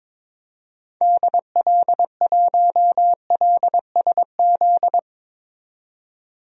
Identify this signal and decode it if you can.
Morse code, heard as a keyed tone, about 22 wpm, 700 Hz. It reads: DL1LSZ
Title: DXCC-100 Call Signs - 6 Characters - 22wpm